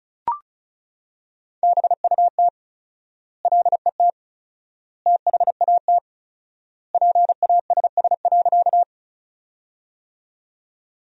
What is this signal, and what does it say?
Morse code, heard as a keyed tone, about 36 wpm, 700 Hz. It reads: BUT LET THAT PASS.
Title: 4x Word-Spacing - Sentences from Top 400 Words - 35wpm